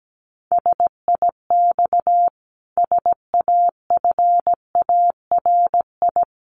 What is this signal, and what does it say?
Morse code, heard as a keyed tone, about 17 wpm, 700 Hz. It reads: SIX SAFARI
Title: A Sound Of Thunder 17wpm